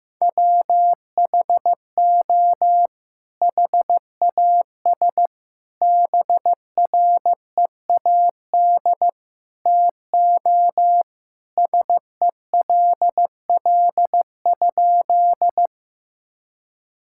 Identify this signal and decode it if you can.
Morse code, heard as a keyed tone, about 15 wpm, 700 Hz. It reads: WHO HAS BREAD TO SELL?
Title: Sentences from Top 1600 Words 15wpm